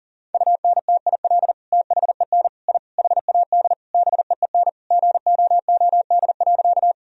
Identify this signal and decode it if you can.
Morse code, heard as a keyed tone, about 40 wpm, 700 Hz. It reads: UNTIL THEN I HAD BEEN GOOD.
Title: Sentences from Top 1000 Words - Review Part 1 - 40wpm